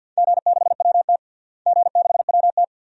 Morse code, heard as a keyed tone, about 52 wpm, 700 Hz. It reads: G6WT G6WT
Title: DXCC-100 Call Signs - 4 Characters - Repeated 2x - 50wpm